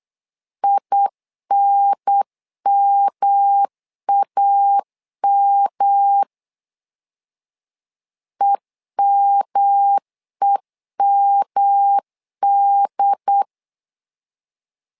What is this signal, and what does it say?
Morse code, heard as a keyed tone, about 8 wpm, 800 Hz. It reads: INMAM EMEMD
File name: LEZIONI / 3 MWRO / 3 M 8wpm 2300